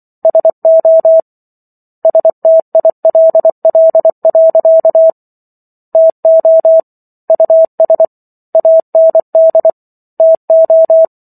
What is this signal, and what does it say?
Morse code, heard as a keyed tone, about 24 wpm, 650 Hz. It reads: SO STILL. TO US AND TO